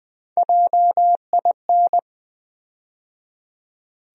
Morse code, heard as a keyed tone, about 20 wpm, 700 Hz. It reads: JIN
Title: USA Call Sign Postfixes - 3 Letters 20wpm